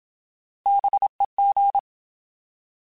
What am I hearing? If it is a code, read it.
Morse code, heard as a keyed tone, about 27 wpm, 800 Hz. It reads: BEG
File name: Alphabetical / Alphabetical mp3s / Bs @25